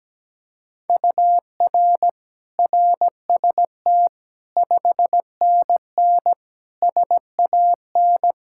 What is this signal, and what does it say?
Morse code, heard as a keyed tone, about 17 wpm, 700 Hz. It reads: UR RST 5NN SAN
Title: Practice QSO - Exchange 1 - Beginning - 17wpm